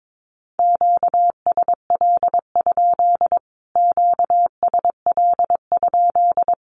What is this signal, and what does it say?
Morse code, heard as a keyed tone, about 22 wpm, 700 Hz. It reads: QSL? QSL?
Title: Q-Codes Repeated 3x 22wpm